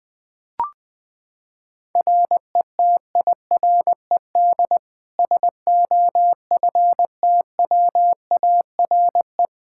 Morse code, heard as a keyed tone, about 20 wpm, 700 Hz. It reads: RETIRED SOFTWARE
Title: Practice QSO - Exchange 3 - More Info - 20wpm